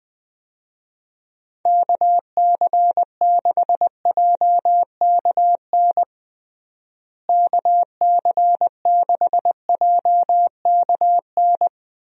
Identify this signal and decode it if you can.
Morse code, heard as a keyed tone, about 20 wpm, 700 Hz. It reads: KC6JKN KC6JKN
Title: USA Call Signs - 6 Characters - Repeated 2x - 20wpm